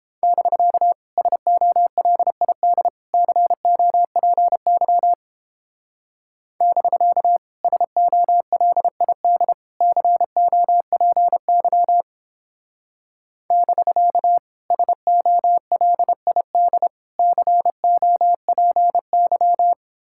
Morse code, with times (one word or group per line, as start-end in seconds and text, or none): 0.2-0.9 <BK>
1.2-2.9 SOLID
3.1-5.1 COPY
6.6-7.4 <BK>
7.6-9.5 SOLID
9.8-12.0 COPY
13.5-14.4 <BK>
14.7-16.9 SOLID
17.2-19.7 COPY